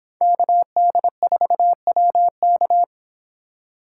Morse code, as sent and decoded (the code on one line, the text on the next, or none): -.- -.. ....- .-- -.-
KD4WK